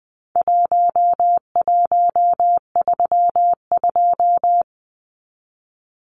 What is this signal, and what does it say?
Morse code, heard as a keyed tone, about 20 wpm, 700 Hz. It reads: 1132
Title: Numbers - Contest Style - 4-Digit 20wpm